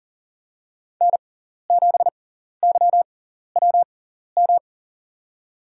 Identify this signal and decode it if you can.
Morse code, heard as a keyed tone, about 38 wpm, spaced as for 14 wpm, 700 Hz. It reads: N7YWM
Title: USA Call Signs - 6x Character Spacing - 40wpm